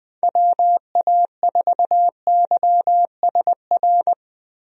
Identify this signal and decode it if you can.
Morse code, heard as a keyed tone, about 20 wpm, 700 Hz. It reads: WA4YSR
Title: USA Call Signs - 6 Characters - Repeated 3x - 20wpm